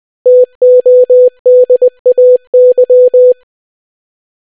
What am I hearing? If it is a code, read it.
Morse code, heard as a keyed tone, about 20 wpm, 500 Hz. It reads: TODAY